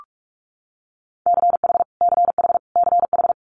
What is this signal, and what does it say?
Morse code, heard as a keyed tone, about 46 wpm, 700 Hz. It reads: CH CH CH
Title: Multi-letter Phonemes Repeated 3x 45wpm